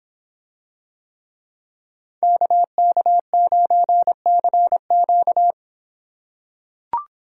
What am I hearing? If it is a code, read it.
Morse code, heard as a keyed tone, about 24 wpm, 700 Hz. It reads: KK9CQ